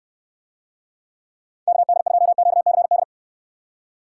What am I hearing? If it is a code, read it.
Morse code, heard as a keyed tone, about 62 wpm, 700 Hz. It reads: KD2QCD